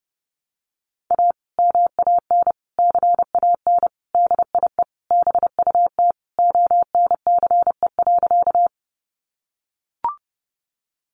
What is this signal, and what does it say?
Morse code, heard as a keyed tone, about 30 wpm, 700 Hz. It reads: A MAN CAN DIE BUT ONCE.
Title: Sentences from Top 1300 Words 30wpm